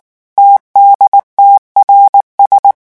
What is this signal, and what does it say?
Morse code, heard as a keyed tone, about 19 wpm, 800 Hz. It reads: TDTRS
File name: morse mp3 / revision7-vitesse 18 mots